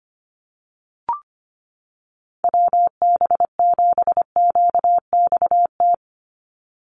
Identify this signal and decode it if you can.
Morse code, heard as a keyed tone, about 25 wpm, 700 Hz. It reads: WB7QXT
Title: USA Call Signs 25wpm